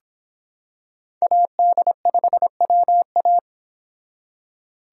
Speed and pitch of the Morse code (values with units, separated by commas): 25 wpm, 700 Hz